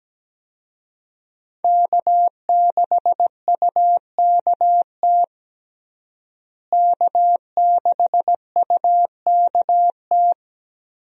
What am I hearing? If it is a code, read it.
Morse code, heard as a keyed tone, about 17 wpm, 700 Hz. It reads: K6UKT K6UKT
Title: USA Call Signs - 5 Characters - Repeated 3x - 17wpm